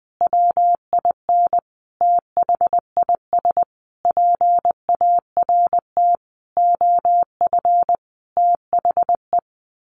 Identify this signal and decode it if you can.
Morse code, heard as a keyed tone, about 20 wpm, 700 Hz. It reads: WIN THIS PART OF THE
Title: Sentences from Top 1500 Words 20wpm